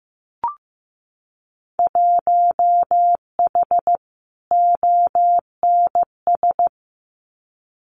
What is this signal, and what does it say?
Morse code, heard as a keyed tone, about 15 wpm, 700 Hz. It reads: 1H ONS
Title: Winter Field Day Exchanges 15wpm